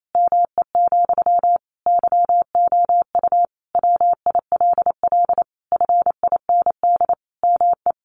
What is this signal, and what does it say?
Morse code, heard as a keyed tone, about 28 wpm, 700 Hz. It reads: ME, YOU WILL FIND ME
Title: Sentences from Top 1200 Words 28wpm